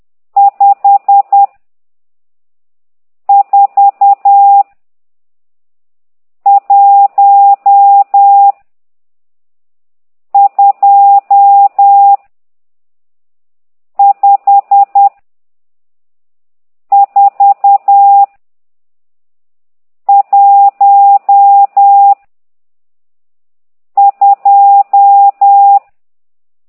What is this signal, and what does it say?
Morse code, heard as a keyed tone, about 10 wpm, 800 Hz. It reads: 5 4 1 2 5 4 1 2
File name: ct2idw / exerc12